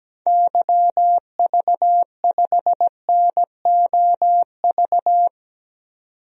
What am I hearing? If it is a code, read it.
Morse code, heard as a keyed tone, about 17 wpm, 700 Hz. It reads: YV5NOV
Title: DXCC-100 Call Signs - 6 Characters - 17wpm